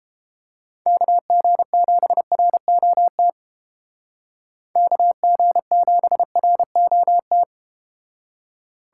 Morse code, heard as a keyed tone, about 32 wpm, 700 Hz. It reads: KG7ROT KG7ROT